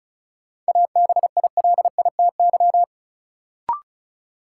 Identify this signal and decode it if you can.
Morse code, heard as a keyed tone, about 36 wpm, 700 Hz. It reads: ABILITY
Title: Sentences from Top 1000 Words - Review Part 3 - New Word Drill - 35wpm